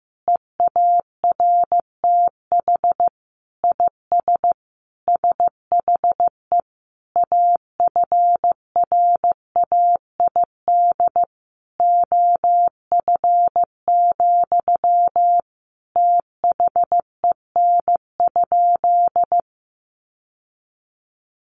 Morse code, heard as a keyed tone, about 14 wpm, 700 Hz. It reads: EARTH IS SHE AFRAID OF, THEN?
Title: Sentences from Top 1500 Words 15wpm